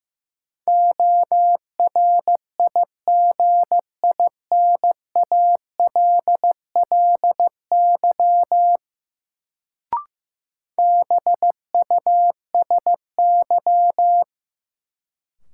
Morse code, heard as a keyed tone, about 15 wpm, 700 Hz. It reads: ORIGINALLY BUSY
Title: Sentences from Top 1500 Words - New Word Drill Alt - 15wpm